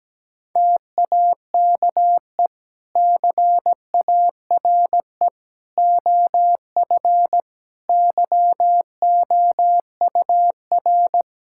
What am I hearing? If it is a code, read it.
Morse code, heard as a keyed tone, about 17 wpm, 700 Hz. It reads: TAKE CARE OF YOUR
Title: Sentences from Top 1000 Words - Review Part 5 - 17wpm